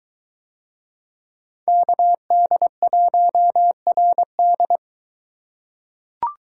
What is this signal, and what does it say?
Morse code, heard as a keyed tone, about 23 wpm, 700 Hz. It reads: KD1RD